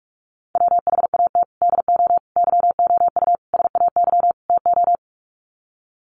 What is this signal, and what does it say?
Morse code, heard as a keyed tone, about 44 wpm, 700 Hz. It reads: WHAT DO YOU SAY TO
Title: Sets of 5 Words 45wpm